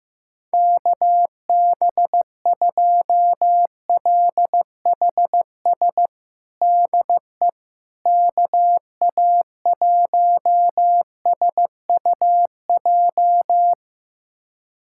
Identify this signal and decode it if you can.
Morse code, heard as a keyed tone, about 15 wpm, 700 Hz. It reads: KB2LHS DE KA1SUJ
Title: USA Callsign DE Callsign 15wpm